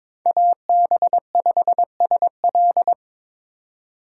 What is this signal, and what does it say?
Morse code, heard as a keyed tone, about 22 wpm, 700 Hz. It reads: AB5SL
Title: USA Call Signs - 5 Characters - 3x Character Spacing - 22wpm